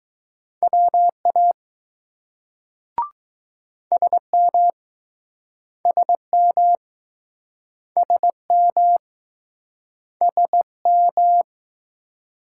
Morse code, heard as a keyed tone, about 20 wpm, 700 Hz. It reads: WA SM SM SM SM